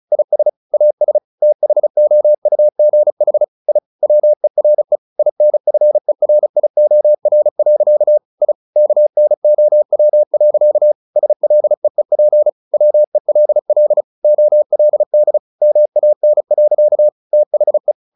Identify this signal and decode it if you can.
Morse code, heard as a keyed tone, about 34 wpm, 600 Hz. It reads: IS AS THOUGH I WERE INFERIOR. I KNOW. SLEEP WELL OLD MAN. THE
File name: the old man and the sea / the old man and the sea 1 35wpm